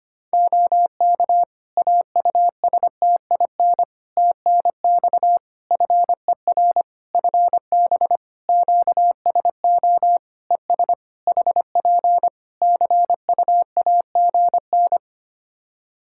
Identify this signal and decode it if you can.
Morse code, heard as a keyed tone, about 25 wpm, 700 Hz. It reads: OK AUSTIN TNX FER FB QSO ES HP CUAGN
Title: Practice QSO - Exchange 4 - Ending - 25wpm